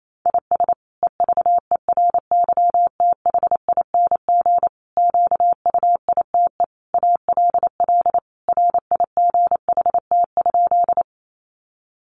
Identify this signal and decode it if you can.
Morse code, heard as a keyed tone, about 28 wpm, 700 Hz. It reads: IS EVERYTHING QUITE ALL RIGHT?
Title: Sentences from Top 700 Words 28wpm